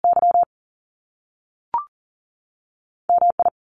Morse code, heard as a keyed tone, about 41 wpm, 700 Hz. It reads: Y MI
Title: US States 40wpm